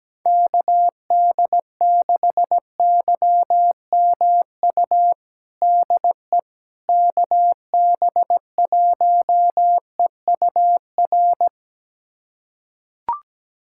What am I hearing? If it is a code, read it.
Morse code, heard as a keyed tone, about 17 wpm, 700 Hz. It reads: KD6YMU DE KB1EUR
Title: USA Callsign DE Callsign - 6 Characters - 17wpm